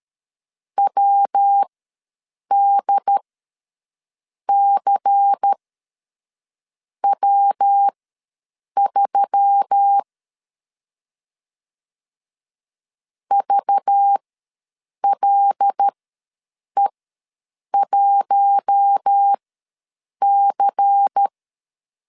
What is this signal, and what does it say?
Morse code, heard as a keyed tone, about 13 wpm, spaced as for 6 wpm, 800 Hz. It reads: WDCW3 VLE1C
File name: LEZIONI / 5 LBVC / 5 LBVC ws 12wpm 3159